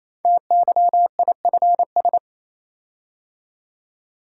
Morse code, heard as a keyed tone, about 28 wpm, 690 Hz. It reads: TYIFS
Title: Code Groups - 5 Letter:Numbers - 28wpm